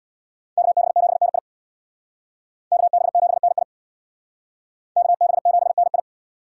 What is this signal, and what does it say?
Morse code, heard as a keyed tone, about 64 wpm, 700 Hz. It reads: KB7NI KB7NI KB7NI